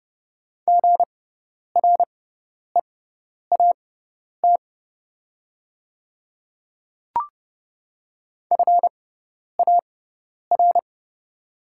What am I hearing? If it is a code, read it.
Morse code, heard as a keyed tone, about 30 wpm, spaced as for 10 wpm, 690 Hz. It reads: GREAT FAR
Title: 6x Character Spacing - Top 500 Words - 30wpm